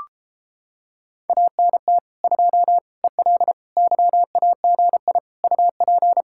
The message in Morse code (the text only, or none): ANT 2 EL YAGI UP